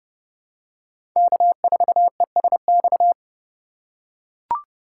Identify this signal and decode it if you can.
Morse code, heard as a keyed tone, about 30 wpm, 690 Hz. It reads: K4ESX
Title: USA Call Signs - 5 Characters - 2x Character Spacing - 30wpm